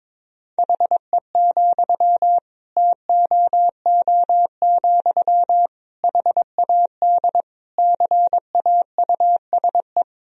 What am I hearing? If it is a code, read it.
Morse code, heard as a keyed tone, about 22 wpm, 700 Hz. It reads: HE, TOO, HAD CAUSE
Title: Sentences from Top 1000 Words - Review Part 3 - 22wpm